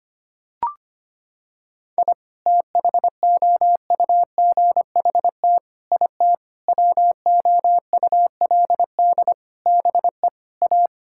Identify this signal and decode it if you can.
Morse code, heard as a keyed tone, about 25 wpm, 700 Hz. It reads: I THOUGHT IT WOULD BE A
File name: Sets of 9 Words - 25wpm